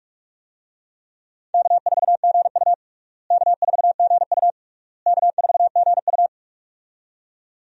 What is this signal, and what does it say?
Morse code, heard as a keyed tone, about 46 wpm, 700 Hz. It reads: K4GU K4GU K4GU
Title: USA Call Signs - 4 Characters - Repeated 3x - 45wpm